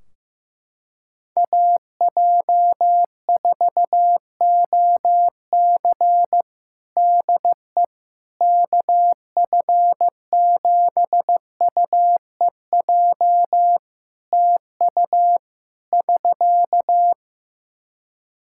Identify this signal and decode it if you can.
Morse code, heard as a keyed tone, about 15 wpm, 700 Hz. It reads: AJ4OC DE KF7UEJ TU <SK>
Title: Practice QSO - Exchange 4 - Ending - 15wpm